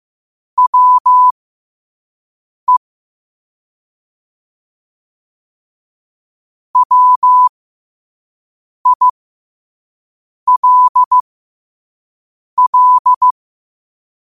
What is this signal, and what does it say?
Morse code, heard as a keyed tone, about 15 wpm, spaced as for 5 wpm, 1000 Hz. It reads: WE WILL
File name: qotdfiles / 5wpm